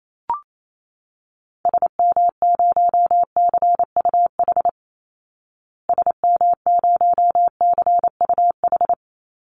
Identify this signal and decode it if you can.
Morse code, heard as a keyed tone, about 28 wpm, 700 Hz. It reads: SM0CUH SM0CUH
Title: Call Signs Repeated 3x - 28wpm